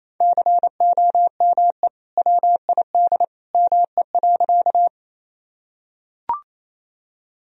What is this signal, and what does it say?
Morse code, heard as a keyed tone, about 28 wpm, 700 Hz. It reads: COME WID ME.
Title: Sentences from Top 200 Words and Abbreviations 28wpm